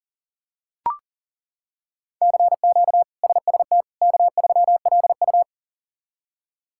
Morse code, heard as a keyed tone, about 37 wpm, 700 Hz. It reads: CQ SST K3LU
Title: K1USN SST QSO Practice - 40wpm